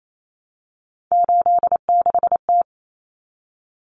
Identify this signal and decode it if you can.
Morse code, heard as a keyed tone, about 28 wpm, 700 Hz. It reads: 86T
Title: Common Cut Numbers - 3-Digit 28wpm